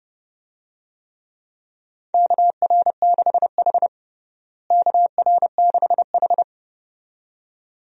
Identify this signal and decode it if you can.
Morse code, heard as a keyed tone, about 31 wpm, 700 Hz. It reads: KR6H KR6H
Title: USA Call Signs - 4 Characters - Repeated 2x - 30wpm